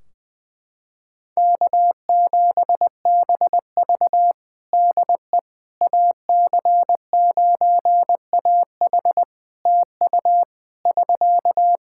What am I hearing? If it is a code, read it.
Morse code, heard as a keyed tone, about 20 wpm, 700 Hz. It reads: K7BV DE AC9AH TU <SK>
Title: Practice QSO - Exchange 4 - Ending - 20wpm